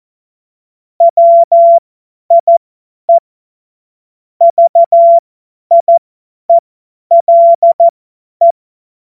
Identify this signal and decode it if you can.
Morse code, heard as a keyed tone, about 14 wpm, spaced as for 10 wpm, 680 Hz. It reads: WIE VIELE
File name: Morsekurs / Witze / Witz 13. April 2021, 50 F70